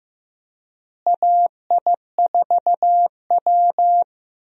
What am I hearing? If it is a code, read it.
Morse code, heard as a keyed tone, about 16 wpm, 700 Hz. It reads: AI4W
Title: USA Call Signs - 4 Characters - Repeated 2x - 15wpm